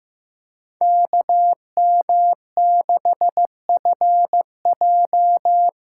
Morse code, heard as a keyed tone, about 15 wpm, 700 Hz. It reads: KM6FJ
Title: USA Call Signs - 6 Characters - 8x Character Spacing - 15wpm